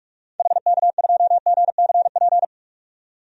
Morse code, heard as a keyed tone, about 41 wpm, 700 Hz. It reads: HK2GCP
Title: DXCC-100 Call Signs - 6 Characters - 45wpm